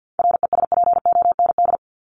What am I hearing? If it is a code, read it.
Morse code, heard as a keyed tone, about 50 wpm, 700 Hz. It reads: RESPOND
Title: Top 1751-2000 Words 50wpm